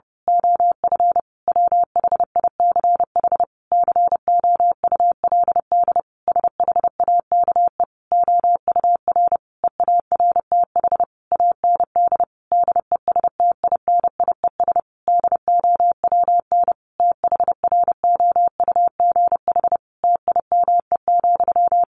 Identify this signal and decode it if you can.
Morse code, heard as a keyed tone, about 30 wpm, 700 Hz. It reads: OF WHICH COULD SHAKE OUR EARTH AND DESTINIES DOWN THROUGH TIME,
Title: A Sound Of Thunder 30wpm